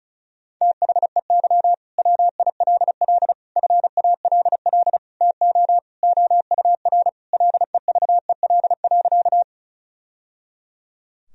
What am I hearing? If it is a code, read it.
Morse code, heard as a keyed tone, about 34 wpm, 700 Hz. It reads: THEY WILL FALL TO OUR LEVEL.
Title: Sentences from Top 1000 Words - Review Part 4 - 35wpm